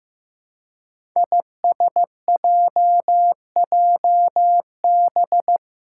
Morse code, heard as a keyed tone, about 15 wpm, 700 Hz. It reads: ISJJB
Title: Five Challenging Letter-Number Combinations 15wpm